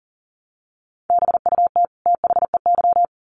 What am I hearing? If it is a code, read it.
Morse code, heard as a keyed tone, about 40 wpm, 700 Hz. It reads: BUT THEY
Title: Sets of 2 Words 40wpm